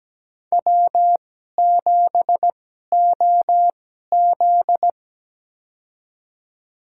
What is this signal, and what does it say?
Morse code, heard as a keyed tone, about 17 wpm, spaced as for 12 wpm, 700 Hz. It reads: W7OZ
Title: USA Call Signs - 2x Character Spacing - 17wpm